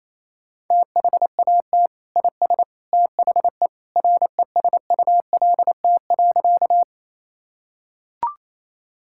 Morse code, heard as a keyed tone, about 28 wpm, 700 Hz. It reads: THAT IS THE RESULT.
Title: Sentences from Top 1000 Words - Review Part 5 - 28wpm